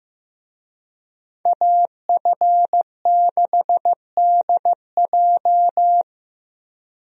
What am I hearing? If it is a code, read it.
Morse code, heard as a keyed tone, about 15 wpm, 700 Hz. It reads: AF6DJ
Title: USA Call Signs - 5 Characters - Repeated 3x - 15wpm